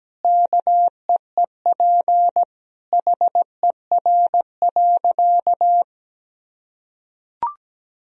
Morse code, heard as a keyed tone, about 17 wpm, 700 Hz. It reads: KEEP HER.
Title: Sentences from Top 1000 Words - Review Part 2 - 17wpm